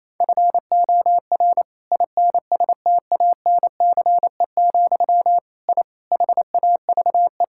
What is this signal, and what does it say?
Morse code, heard as a keyed tone, about 27 wpm, 700 Hz. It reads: FOR INSTANCE, I HAVE
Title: Sentences from Top 1000 Words - Review Part 5 - 28wpm